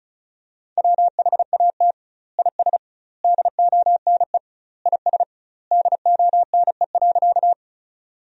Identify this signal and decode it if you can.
Morse code, heard as a keyed tone, about 34 wpm, 700 Hz. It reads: WHAT IS DONE IS DONE.
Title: 2x Word-Spacing - Sentences from Top 500 Words - 35wpm